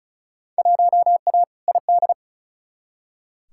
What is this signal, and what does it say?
Morse code, heard as a keyed tone, about 36 wpm, 700 Hz. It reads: 1A ID
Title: Field Day Exchanges - 35wpm